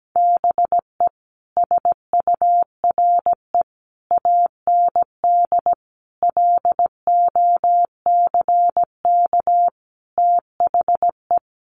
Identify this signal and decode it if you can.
Morse code, heard as a keyed tone, about 17 wpm, 700 Hz. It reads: BE SURE AND LOCK THE
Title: Sentences from Top 1300 Words 17wpm